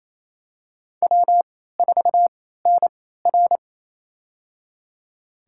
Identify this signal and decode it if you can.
Morse code, heard as a keyed tone, about 27 wpm, spaced as for 16 wpm, 700 Hz. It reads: W4NR
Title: USA Call Signs - 4 Characters - 3x Character Spacing - 28wpm